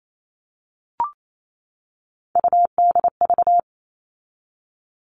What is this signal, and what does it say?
Morse code, heard as a keyed tone, about 28 wpm, 700 Hz. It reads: UDV
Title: All Cut Numbers - 3-Digit 28wpm